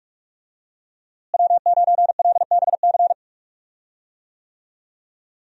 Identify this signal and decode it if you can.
Morse code, heard as a keyed tone, about 46 wpm, 700 Hz. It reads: W9LBC
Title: USA Call Signs - 5 Characters - Repeated 3x - 45wpm